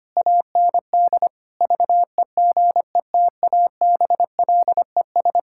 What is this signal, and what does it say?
Morse code, heard as a keyed tone, about 25 wpm, 700 Hz. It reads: AND VEGETABLES
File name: practice / 25wpm / Difficult Sentences with Hints - 25wpm